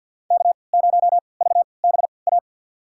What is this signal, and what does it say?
Morse code, heard as a keyed tone, about 46 wpm, spaced as for 27 wpm, 700 Hz. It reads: K0VBA
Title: USA Call Signs - 5 Characters - 3x Character Spacing - 50wpm